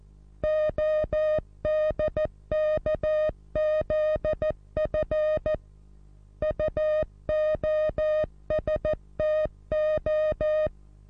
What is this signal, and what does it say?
Morse code, heard as a keyed tone, about 14 wpm, 620 Hz. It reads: ODKZF UOSTO
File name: repos / Morsekurs / DJ4UF Morsen27